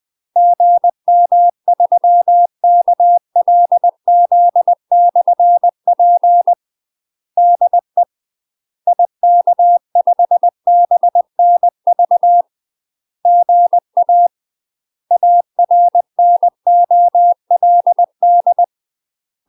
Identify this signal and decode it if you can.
Morse code, heard as a keyed tone, about 20 wpm, 700 Hz. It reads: GM3KLZ/P DE IK5BNV GA ARNOLD
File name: UK MASTER QSOs TRAINING / 20WPM / UK MASTER QSOs 0007